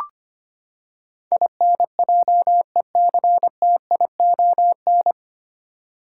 Morse code, heard as a keyed tone, about 25 wpm, 700 Hz. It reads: INJECTION
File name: Nine Letter Words - 25wpm